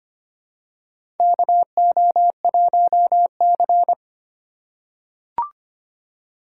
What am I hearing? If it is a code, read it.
Morse code, heard as a keyed tone, about 25 wpm, 700 Hz. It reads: KO1C
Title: USA Call Signs - 4x Character Spacing - 25wpm